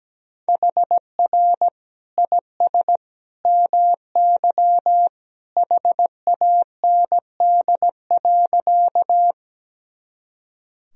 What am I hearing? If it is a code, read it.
Morse code, heard as a keyed tone, about 17 wpm, 700 Hz. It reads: HR IS MY HAND.
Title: Sentences from Top 200 Words and Abbreviations 17wpm